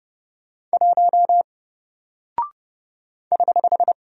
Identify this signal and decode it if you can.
Morse code, heard as a keyed tone, about 30 wpm, 700 Hz. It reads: 1 <HH>
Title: Single Character (Letter:Number:Punctuation:Prosign) Alt 30wpm